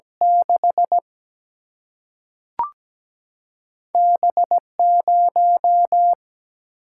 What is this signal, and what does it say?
Morse code, heard as a keyed tone, about 17 wpm, 700 Hz. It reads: 6 B0
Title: Single-Letter Single-Number 17wpm